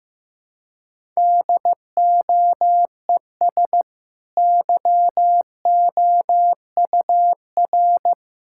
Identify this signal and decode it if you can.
Morse code, heard as a keyed tone, about 14 wpm, 700 Hz. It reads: DOES YOUR
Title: Sentences from Top 1000 Words - Review Part 6 - 15wpm